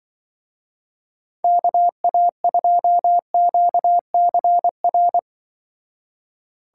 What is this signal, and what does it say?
Morse code, heard as a keyed tone, about 25 wpm, 700 Hz. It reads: KA2QCR